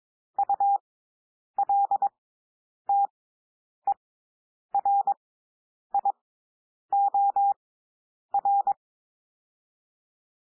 Morse code, heard as a keyed tone, about 22 wpm, spaced as for 9 wpm, 800 Hz. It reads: ULTERIOR
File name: SOUND WORDS 22 wpm